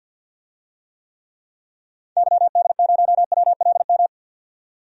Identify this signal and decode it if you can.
Morse code, heard as a keyed tone, about 51 wpm, 700 Hz. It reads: YD0WLM
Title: DXCC-100 Call Signs - 6 Characters - 50wpm